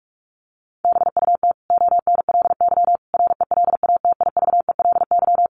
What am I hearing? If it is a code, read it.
Morse code, heard as a keyed tone, about 44 wpm, 700 Hz. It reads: BUT ONLY RELATIVELY
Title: Sentences from Top 1100 Words 45wpm